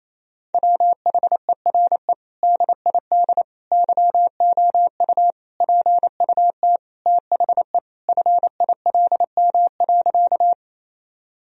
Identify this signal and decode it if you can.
Morse code, heard as a keyed tone, about 28 wpm, 700 Hz. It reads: WHERE DID YOU PUT THE FILM.
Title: Sentences from Top 1000 Words - Review Part 3 - 28wpm